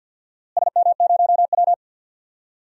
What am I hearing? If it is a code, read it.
Morse code, heard as a keyed tone, about 51 wpm, 700 Hz. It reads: SM0W
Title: DXCC-100 Call Signs - 4 Characters - Repeated 2x - 50wpm